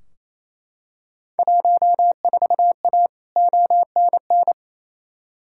Morse code, heard as a keyed tone, about 28 wpm, 700 Hz. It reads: 14A ONN
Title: Field Day Exchanges Encore - 28wpm